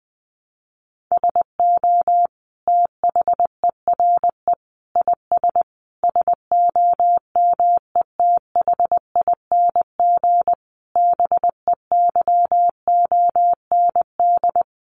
Morse code, heard as a keyed tone, about 20 wpm, 700 Hz. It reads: SO THERE IS SOMETHING BEYOND
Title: Sentences from Top 1500 Words 20wpm